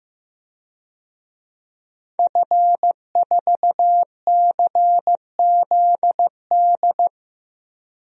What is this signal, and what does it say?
Morse code, heard as a keyed tone, about 15 wpm, 700 Hz. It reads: F4CZD
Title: DXCC-100 Call Signs - 5 Characters - Repeated 2x - 15wpm